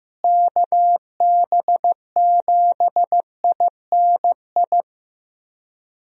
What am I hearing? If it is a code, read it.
Morse code, heard as a keyed tone, about 15 wpm, 700 Hz. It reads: KB7INI
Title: USA Call Signs - 6 Characters - Repeated 2x - 15wpm